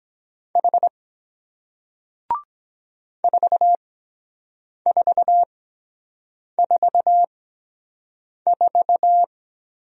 Morse code, heard as a keyed tone, about 23 wpm, spaced as for 7 wpm, 700 Hz. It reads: H 4444